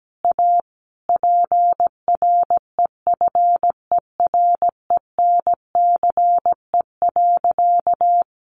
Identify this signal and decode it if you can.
Morse code, heard as a keyed tone, about 17 wpm, 700 Hz. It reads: A PREFERENCE.
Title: Sentences from Top 1500 Words 17wpm